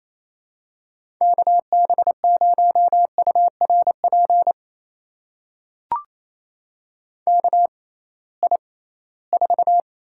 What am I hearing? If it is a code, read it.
Morse code, heard as a keyed tone, about 28 wpm, spaced as for 18 wpm, 700 Hz. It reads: KB0URP KI4
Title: USA Call Signs - 6 Characters - 6x Character Spacing - 28wpm